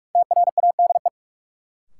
Morse code, heard as a keyed tone, about 42 wpm, 700 Hz. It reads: TRADE
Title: Sentences from Top 600 Words - New Word Drill - 45wpm